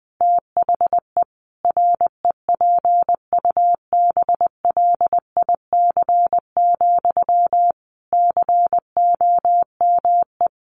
Morse code, heard as a keyed tone, about 20 wpm, 700 Hz. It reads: THE REPUBLIC, COME
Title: Sentences from Top 1500 Words 20wpm